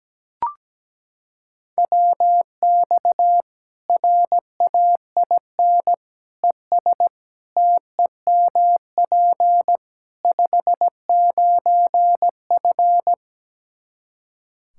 Morse code, heard as a keyed tone, about 17 wpm, 700 Hz. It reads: WX RAIN ES TEMP 59F
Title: Practice QSO - Exchange 2 - Info - 17wpm